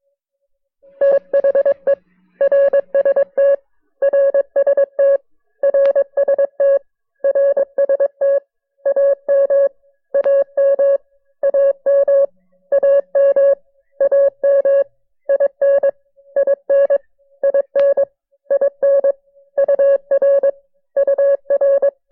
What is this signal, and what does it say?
Morse code, heard as a keyed tone, about 20 wpm, 590 Hz. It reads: THE RST RST RST RST AM AM AM AM AM IN IN IN IN UR UR
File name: FreeFiles / CW-practice / KB6MT-tape1-side2